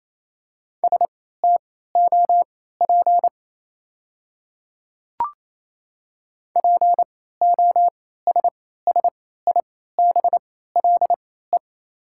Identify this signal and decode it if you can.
Morse code, heard as a keyed tone, about 28 wpm, spaced as for 16 wpm, 690 Hz. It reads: STOP POSSIBLE
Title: 3x Character Spacing - Top 500 Words - 28wpm